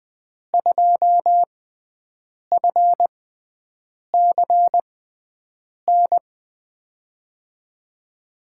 Morse code, heard as a keyed tone, about 21 wpm, spaced as for 7 wpm, 700 Hz. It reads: 2FCN
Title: USA Call Signs - 6 Characters - 6x Character Spacing - 20wpm